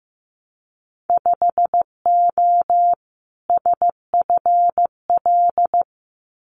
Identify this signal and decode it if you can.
Morse code, heard as a keyed tone, about 15 wpm, 700 Hz. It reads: 5O SFL
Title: Winter Field Day Exchanges 15wpm